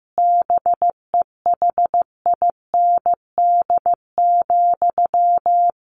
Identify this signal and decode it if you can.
Morse code, heard as a keyed tone, about 15 wpm, 700 Hz. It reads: BEHIND,
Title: Sentences from Top 1500 Words 15wpm